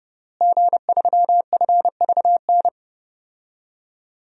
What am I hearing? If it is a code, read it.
Morse code, heard as a keyed tone, about 30 wpm, 700 Hz. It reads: G3FVN
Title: DXCC-100 Call Signs - 5 Characters - Repeated 3x - 30wpm